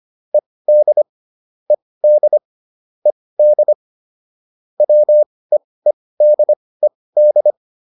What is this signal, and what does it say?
Morse code, heard as a keyed tone, about 25 wpm, spaced as for 18 wpm, 600 Hz. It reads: ED ED ED WEEDED